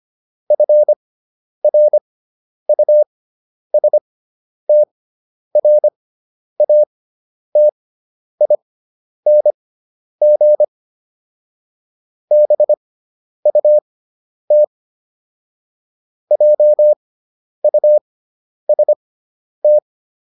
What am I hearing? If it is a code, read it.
Morse code, heard as a keyed tone, about 25 wpm, spaced as for 10 wpm, 600 Hz. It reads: FRUSTRATING BUT JUST